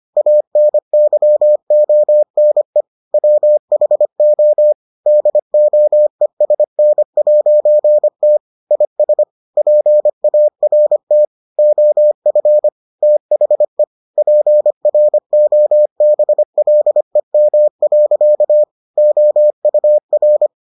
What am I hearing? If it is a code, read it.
Morse code, heard as a keyed tone, about 25 wpm, 600 Hz. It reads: ANYONE WHO DOESN'T IS PART OF THE PROBLEM. OUR